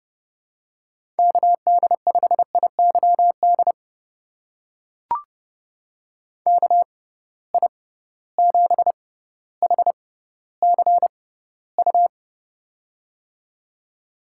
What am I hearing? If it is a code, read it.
Morse code, heard as a keyed tone, about 30 wpm, spaced as for 15 wpm, 700 Hz. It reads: KD5IYD KI7HCU
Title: USA Call Signs - 6 Characters - 6x Character Spacing - 30wpm